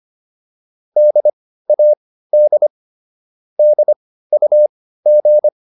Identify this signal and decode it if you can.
Morse code, heard as a keyed tone, about 25 wpm, spaced as for 15 wpm, 600 Hz. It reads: DAD DUG